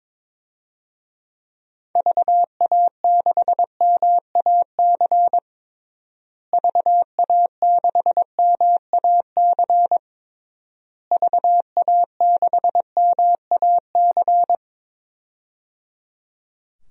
Morse code, heard as a keyed tone, about 21 wpm, 700 Hz. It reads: VA6MAC VA6MAC VA6MAC
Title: DXCC-100 Call Signs - 6 Characters - Repeated 3x - 22wpm